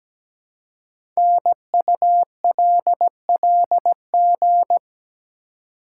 Morse code, heard as a keyed tone, about 17 wpm, 700 Hz. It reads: NULLG
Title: Code Groups - 5 Letter:Numbers - 17wpm